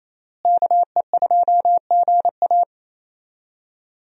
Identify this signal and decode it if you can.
Morse code, heard as a keyed tone, about 27 wpm, 700 Hz. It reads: KE2GA
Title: USA Call Signs - 5 Characters - Repeated 3x - 28wpm